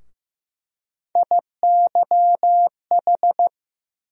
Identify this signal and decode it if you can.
Morse code, heard as a keyed tone, about 15 wpm, 700 Hz. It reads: IYH
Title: USA Call Sign Postfixes - 3 Letters 15wpm